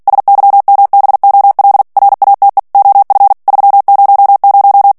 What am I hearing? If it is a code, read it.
Morse code, heard as a keyed tone, about 48 wpm, 800 Hz. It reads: SYMBOL RATE OF 300
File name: ANGLAIS texte clair 3 vitesse 45 mots minute